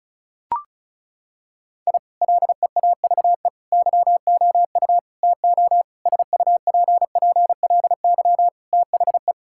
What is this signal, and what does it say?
Morse code, heard as a keyed tone, about 34 wpm, 700 Hz. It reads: I LEAVE YOU TO SUPPLY THE
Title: Sentences from Top 1000 Words - Review Part 5 - 35wpm